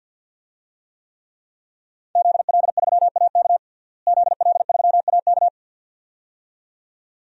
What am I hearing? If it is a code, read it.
Morse code, heard as a keyed tone, about 53 wpm, 700 Hz. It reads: ZL3AK ZL3AK
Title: DXCC-100 Call Signs - 5 Characters - Repeated 2x - 50wpm